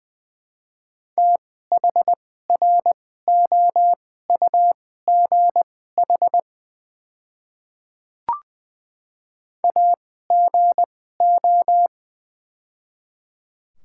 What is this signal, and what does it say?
Morse code, heard as a keyed tone, about 20 wpm, spaced as for 15 wpm, 700 Hz. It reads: THROUGH AGO
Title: 2x Character Spacing - Top 500 Words - 20wpm